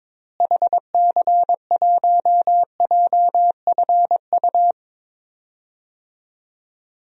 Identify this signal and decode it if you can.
Morse code, heard as a keyed tone, about 22 wpm, 700 Hz. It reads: HC1JFU
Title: DXCC-100 Call Signs - 6 Characters - Repeated 3x - 22wpm